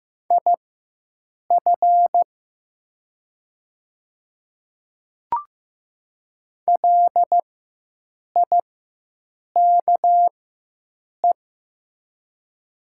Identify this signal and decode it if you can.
Morse code, heard as a keyed tone, about 15 wpm, spaced as for 7 wpm, 700 Hz. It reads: IF LIKE
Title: 4x Character Spacing - Top 100 Words - 15wpm